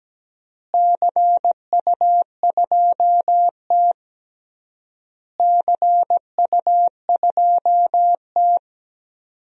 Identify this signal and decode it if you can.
Morse code, heard as a keyed tone, about 17 wpm, 700 Hz. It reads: CU2T CU2T
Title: DXCC-100 Call Signs - 4 Characters - Repeated 3x - 17wpm